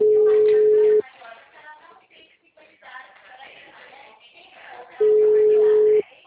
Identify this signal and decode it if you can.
Ringback tone (Europe, 425 Hz), with speech.